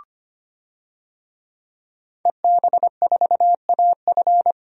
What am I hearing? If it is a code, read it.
Morse code, heard as a keyed tone, about 25 wpm, 700 Hz. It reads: EB4AF
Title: DXCC-100 Call Signs - 5 Characters - 25wpm